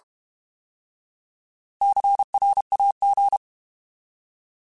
Morse code, heard as a keyed tone, about 32 wpm, 800 Hz. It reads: CRAG
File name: Alphabetical / Alphabetical mp3s / Cs @30